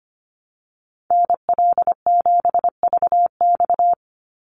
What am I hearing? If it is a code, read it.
Morse code, heard as a keyed tone, about 25 wpm, 700 Hz. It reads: NL7VX
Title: Call Signs - 25wpm